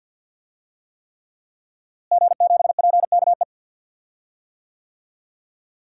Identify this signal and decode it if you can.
Morse code, heard as a keyed tone, about 50 wpm, 700 Hz. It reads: G7PKE
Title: DXCC-100 Call Signs - 5 Characters - Repeated 3x - 50wpm